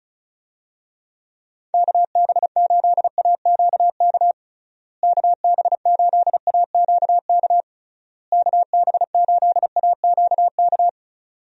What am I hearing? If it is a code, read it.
Morse code, heard as a keyed tone, about 36 wpm, 700 Hz. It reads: KB8AQK KB8AQK KB8AQK
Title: USA Call Signs - 6 Characters - Repeated 3x - 35wpm